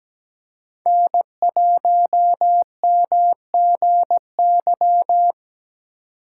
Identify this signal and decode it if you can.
Morse code, heard as a keyed tone, about 17 wpm, 700 Hz. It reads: N1MGY
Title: USA Call Signs - 3x Character Spacing - 17wpm